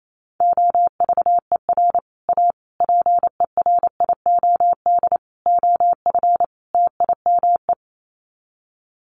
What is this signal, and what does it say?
Morse code, heard as a keyed tone, about 28 wpm, 700 Hz. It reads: OVER A PERIOD OF TIME
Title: Sets of 5 Words Encore2 28wpm